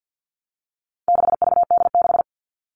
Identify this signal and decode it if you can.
Morse code, heard as a keyed tone, about 50 wpm, 700 Hz. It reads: 6VD6
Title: All Cut Numbers - 4-Digit 50wpm